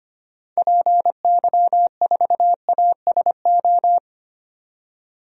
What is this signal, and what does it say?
Morse code, heard as a keyed tone, about 25 wpm, 700 Hz. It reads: PY4ASO
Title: DXCC-100 Call Signs - 6 Characters - 25wpm